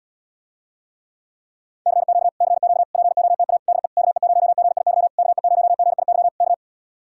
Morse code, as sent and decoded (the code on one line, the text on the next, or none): -.-. --.- / -.-. --.- / .--. --- - .- / -.. / -.- . ----- --. .. -.-- / -.- . ----- --. .. -.-- / -.-
CQ CQ POTA DE KE0GIY KE0GIY K